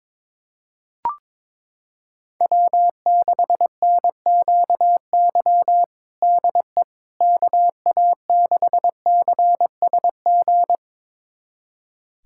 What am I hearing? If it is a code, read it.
Morse code, heard as a keyed tone, about 22 wpm, 700 Hz. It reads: W6NQY DE KA6CSG
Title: USA Callsign DE Callsign 22wpm